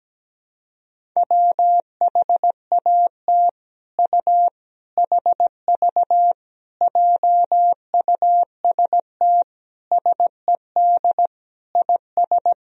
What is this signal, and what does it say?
Morse code, heard as a keyed tone, about 17 wpm, 700 Hz. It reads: WHAT U HV JUST SED IS
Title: Sentences from Top 200 Words and Abbreviations 17wpm